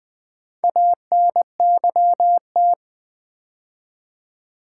Morse code, heard as a keyed tone, about 20 wpm, 700 Hz. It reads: ANYT
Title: Build A Word - 2 Letters At A Time - 8-Letter Words - 20wpm